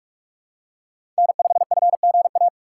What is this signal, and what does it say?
Morse code, heard as a keyed tone, about 43 wpm, 700 Hz. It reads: N5FGA
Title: USA Call Signs - 4x Character Spacing - 45wpm